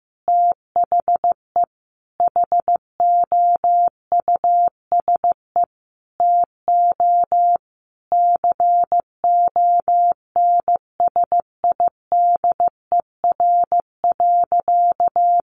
Morse code, heard as a keyed tone, about 14 wpm, 700 Hz. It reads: THE HOUSE TO CONSIDER.
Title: Sentences from Top 900 Words 15wpm